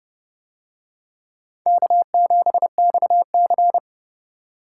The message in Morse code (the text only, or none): K7XC